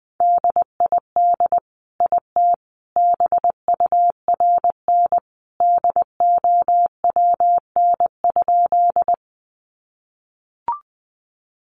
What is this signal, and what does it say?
Morse code, heard as a keyed tone, about 19 wpm, 700 Hz. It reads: DID IT BURN DOWN?
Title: Sentences from Top 1300 Words 20wpm